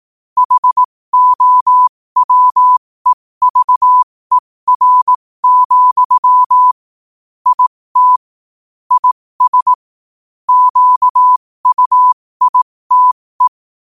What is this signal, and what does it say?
Morse code, heard as a keyed tone, about 18 wpm, spaced as for 15 wpm, 1000 Hz. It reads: HOWEVER, IT IS QUITE
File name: qotdfiles / 15wpm